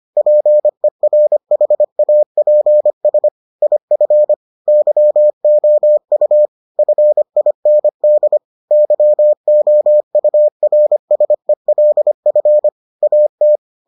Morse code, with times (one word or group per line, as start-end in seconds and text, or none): 0.2-3.3 PERHAPS
3.6-4.3 IF
4.7-6.5 YOU
6.8-8.4 FIND
8.7-12.7 YOURSELF
13.0-13.6 AT